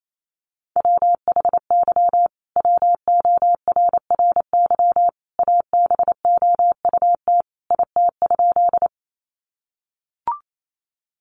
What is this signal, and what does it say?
Morse code, heard as a keyed tone, about 28 wpm, 700 Hz. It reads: WHY WORRY ABOUT IT?
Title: Sentences from Top 1100 Words 28wpm